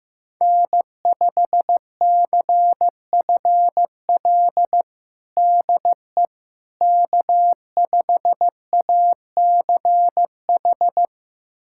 Morse code, heard as a keyed tone, about 16 wpm, 700 Hz. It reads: N5CFL DE K5ACH
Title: USA Callsign DE Callsign - 5 Characters - 15wpm